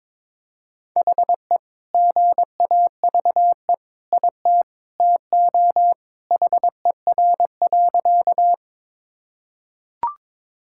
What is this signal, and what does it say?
Morse code, heard as a keyed tone, about 22 wpm, 700 Hz. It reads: HE GAVE IT TO HER.
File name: practice / 22wpm / Sentences with 2-Word Phrases - 22wpm